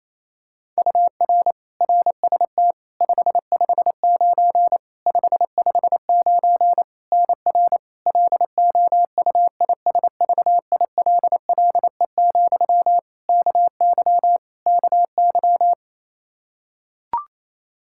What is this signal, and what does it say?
Morse code, heard as a keyed tone, about 28 wpm, 700 Hz. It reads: UR RST 559 559 NR LOUISVILLE, KY KY
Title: Practice QSO - Exchange 1 - Beginning - 28wpm